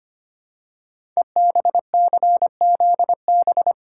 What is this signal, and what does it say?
Morse code, heard as a keyed tone, about 25 wpm, 700 Hz. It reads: EBCZB
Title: Code Groups - 5 Letters - 25wpm